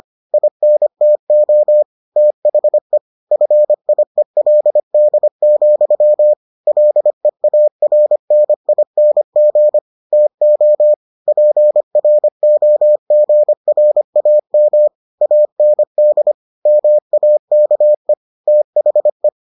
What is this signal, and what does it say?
Morse code, heard as a keyed tone, about 25 wpm, 600 Hz. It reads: INTO THE FIELD, LEARNING TO PROGRAM AND MAKE THE